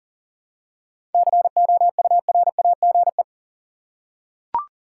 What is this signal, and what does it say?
Morse code, heard as a keyed tone, about 40 wpm, 700 Hz. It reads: COURAGE
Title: Seven Letter Words - 40wpm